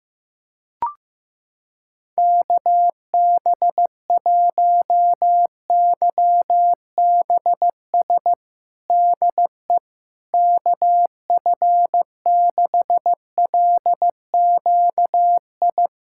Morse code, heard as a keyed tone, about 15 wpm, 700 Hz. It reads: KB1YBS DE KF6LQI
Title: USA Callsign DE Callsign - 6 Characters - 15wpm